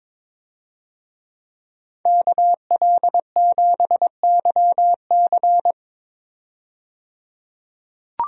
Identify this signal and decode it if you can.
Morse code, heard as a keyed tone, about 22 wpm, 700 Hz. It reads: KL7YC
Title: DXCC-100 Call Signs - 5 Characters - 22wpm